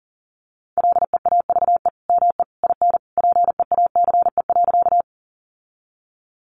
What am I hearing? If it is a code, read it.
Morse code, heard as a keyed tone, about 40 wpm, 700 Hz. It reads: LEAVE ME IN PEACE.
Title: Sentences from Top 1500 Words 40wpm